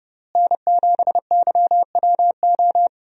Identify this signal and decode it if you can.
Morse code, heard as a keyed tone, about 30 wpm, 700 Hz. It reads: N7YWO
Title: USA Call Signs - 5 Characters - Repeated 3x - 30wpm